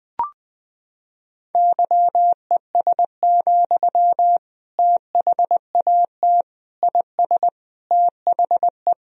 Morse code, heard as a keyed tone, about 21 wpm, 700 Hz. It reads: YES, THAT IS THE
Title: Sentences from Top 1000 Words - Review Part 4 - 20wpm